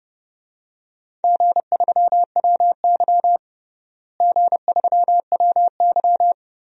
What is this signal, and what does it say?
Morse code, heard as a keyed tone, about 32 wpm, 700 Hz. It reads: G3WY G3WY
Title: DXCC-100 Call Signs - 4 Characters - Repeated 3x - 30wpm